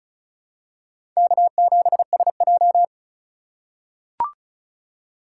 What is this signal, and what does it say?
Morse code, heard as a keyed tone, about 34 wpm, 700 Hz. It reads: K7SJ
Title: USA Call Signs - 4 Characters - 8x Character Spacing - 35wpm